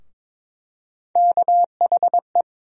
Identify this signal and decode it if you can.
Morse code, heard as a keyed tone, about 21 wpm, 700 Hz. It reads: KHE
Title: USA Call Sign Postfixes - 3 Letters 22wpm